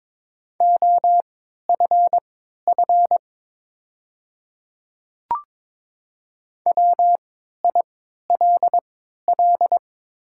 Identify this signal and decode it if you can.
Morse code, heard as a keyed tone, about 22 wpm, spaced as for 12 wpm, 700 Hz. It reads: OFF WILL
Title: 3x Character Spacing - Top 300 Words - 22wpm